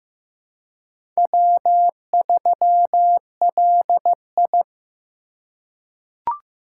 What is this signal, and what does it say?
Morse code, heard as a keyed tone, about 15 wpm, 700 Hz. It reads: W3LI
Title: USA Call Signs - 4 Characters - 3x Character Spacing - 15wpm